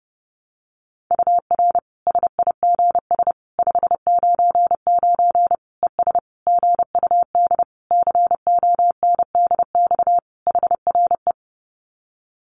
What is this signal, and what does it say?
Morse code, heard as a keyed tone, about 30 wpm, 700 Hz. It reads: UR SIGS 599 ES GUD CONDX HRE
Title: Common QSO Phrases 30wpm